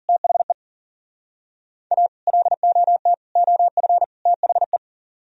Morse code, heard as a keyed tone, about 50 wpm, 700 Hz. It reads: THE A LOT OF THE
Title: Speed-Racing - Sets of 4 Words 40wpm